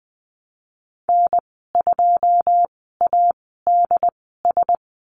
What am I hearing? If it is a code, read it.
Morse code, heard as a keyed tone, about 20 wpm, spaced as for 15 wpm, 700 Hz. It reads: N2ADS
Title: 088 Lesson 15g - Number 5 - Call Signs 20wpm